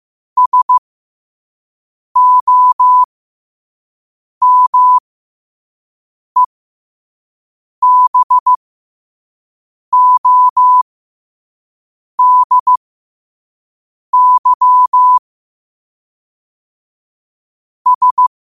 Morse code, heard as a keyed tone, about 15 wpm, spaced as for 5 wpm, 1000 Hz. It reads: SOMEBODY S
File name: qotdfiles / 5wpm